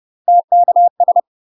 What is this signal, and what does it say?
Morse code, heard as a keyed tone, about 30 wpm, 700 Hz. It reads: TKS
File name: UK MASTER QSOs TRAINING / 30WPM / UK MASTER QSOs 0040